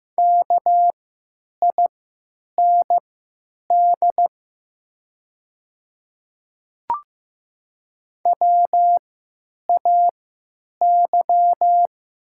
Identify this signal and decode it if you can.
Morse code, heard as a keyed tone, about 15 wpm, spaced as for 9 wpm, 700 Hz. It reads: KIND WAY
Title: 3x Character Spacing - Top 200 Words - 15wpm